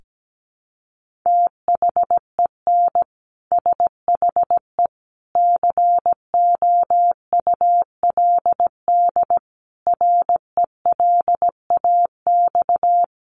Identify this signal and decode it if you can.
Morse code, heard as a keyed tone, about 17 wpm, 700 Hz. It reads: THEN SHE COULD RELAX
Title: Sentences from Top 1400 Words 17wpm